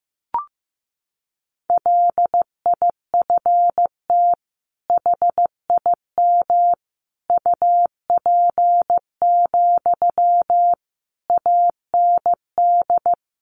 Morse code, heard as a keyed tone, about 15 wpm, 700 Hz. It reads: LIFT HIM UP, AND
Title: Sentences from Top 1000 Words 15wpm